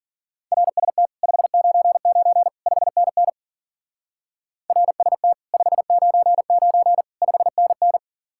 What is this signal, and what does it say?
Morse code, heard as a keyed tone, about 51 wpm, 700 Hz. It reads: RST 599 5NN RST 599 5NN